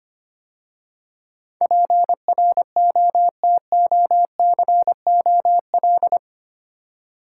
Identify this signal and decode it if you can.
Morse code, heard as a keyed tone, about 25 wpm, 700 Hz. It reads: PROTOCOL
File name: 8 Letter Words - First 3 Letters Given - 25wpm